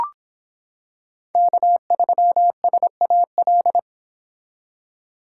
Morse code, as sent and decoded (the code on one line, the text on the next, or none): -.- ...-- ... .- .-..
K3SAL